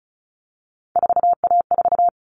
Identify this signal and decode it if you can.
Morse code, heard as a keyed tone, about 34 wpm, 700 Hz. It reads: 4A4
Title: All Cut Numbers - 3-Digit 35wpm